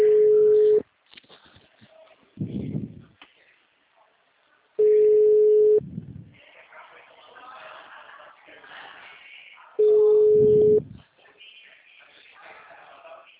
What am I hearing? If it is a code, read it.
Ringback tone (Europe, 425 Hz), with speech.